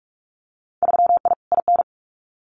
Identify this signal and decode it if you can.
Morse code, heard as a keyed tone, about 46 wpm, 700 Hz. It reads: 3I IN
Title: Winter Field Day Exchanges 45wpm